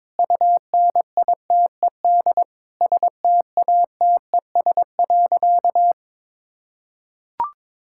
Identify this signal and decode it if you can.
Morse code, heard as a keyed tone, about 22 wpm, 700 Hz. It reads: UNITED STATES.
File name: practice / 22wpm / Sentences with 4-Word Phrases - 22wpm